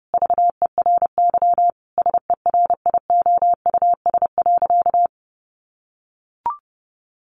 Morse code, heard as a keyed tone, about 30 wpm, 700 Hz. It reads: VERY SERIOUS.
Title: Sentences from Top 1100 Words 30wpm